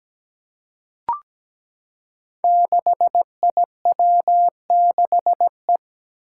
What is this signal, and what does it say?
Morse code, heard as a keyed tone, about 17 wpm, 700 Hz. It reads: 6IW6E
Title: Five Challenging Letter-Number Combinations 17wpm